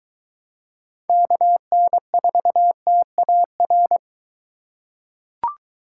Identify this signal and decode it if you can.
Morse code, heard as a keyed tone, about 22 wpm, 700 Hz. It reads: KN4TAR